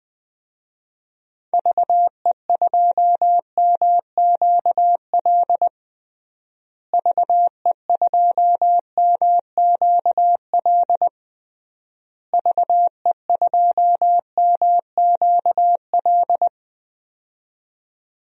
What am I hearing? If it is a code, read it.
Morse code, heard as a keyed tone, about 20 wpm, 700 Hz. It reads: VE2MQL VE2MQL VE2MQL
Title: DXCC-100 Call Signs - 6 Characters - Repeated 3x - 20wpm